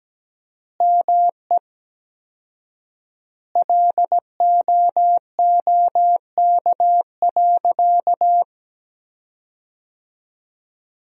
Morse code, heard as a keyed tone, about 17 wpm, 700 Hz. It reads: ME LOOK.
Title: 4x Word-Spacing - Sentences from Top 300 Words - 17wpm